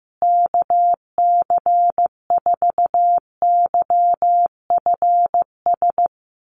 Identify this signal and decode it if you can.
Morse code, heard as a keyed tone, about 15 wpm, 700 Hz. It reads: KC4YFS
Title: USA Call Signs Repeated 3x 15wpm